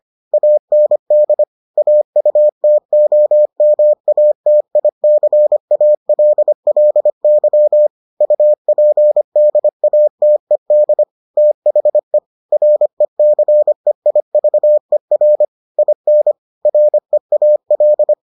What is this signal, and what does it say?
Morse code, heard as a keyed tone, about 25 wpm, 600 Hz. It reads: AND AUTOMATICALLY UPDATED THE RECEIVER IN REAL